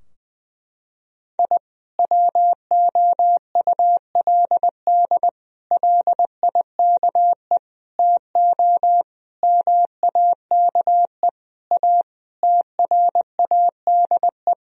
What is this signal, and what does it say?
Morse code, heard as a keyed tone, about 20 wpm, 700 Hz. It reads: I WOULD LIKE TO MAKE A TRADE
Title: Sentences from Top 1000 Words - Review Part 4 - 20wpm